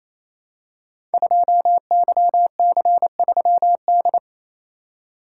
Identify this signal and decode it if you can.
Morse code, heard as a keyed tone, about 29 wpm, 700 Hz. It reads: 2YC3D
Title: Code Groups - 5 Letter:Numbers - 28wpm